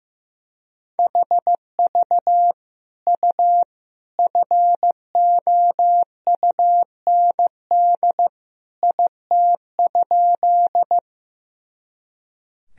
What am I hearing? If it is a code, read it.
Morse code, heard as a keyed tone, about 15 wpm, 700 Hz. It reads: HV U FOUND IT?
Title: Sentences from Top 300 Words and Abbreviations 15wpm